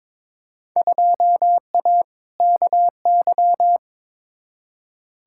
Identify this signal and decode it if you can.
Morse code, heard as a keyed tone, about 22 wpm, 700 Hz. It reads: 2A KY
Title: Field Day Exchanges - 22wpm